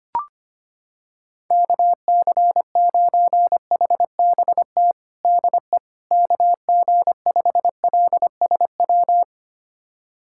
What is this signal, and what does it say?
Morse code, heard as a keyed tone, about 25 wpm, 700 Hz. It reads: KC9HBT DE KG5LSW
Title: USA Callsign DE Callsign 25wpm